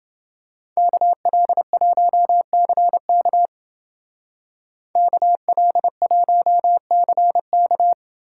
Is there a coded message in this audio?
Yes: Morse code from 0.8 to 7.9 s, heard as a keyed tone, about 28 wpm, 700 Hz. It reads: KL1CK KL1CK